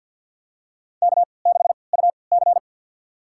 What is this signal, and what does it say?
Morse code, heard as a keyed tone, about 52 wpm, spaced as for 29 wpm, 700 Hz. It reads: K6UC
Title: USA Call Signs - 4 Characters - 3x Character Spacing - 50wpm